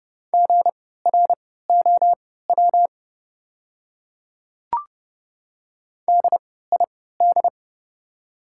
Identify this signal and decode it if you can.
Morse code, heard as a keyed tone, about 30 wpm, spaced as for 17 wpm, 700 Hz. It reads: GROW DID
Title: 3x Character Spacing - Top 300 Words - 30wpm